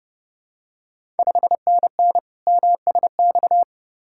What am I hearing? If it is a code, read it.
Morse code, heard as a keyed tone, about 28 wpm, 690 Hz. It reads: 5NN MSX
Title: Ontario QSO Party Exchange 30wpm